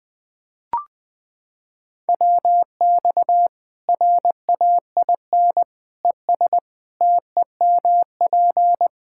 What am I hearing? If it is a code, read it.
Morse code, heard as a keyed tone, about 20 wpm, 700 Hz. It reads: WX RAIN ES TEMP
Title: Practice QSO - Exchange 2 - Info - 20wpm